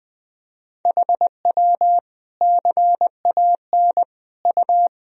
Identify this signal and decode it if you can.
Morse code, heard as a keyed tone, about 21 wpm, 700 Hz. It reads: HW CAN U
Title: Sentences from Top 200 Words and Abbreviations 20wpm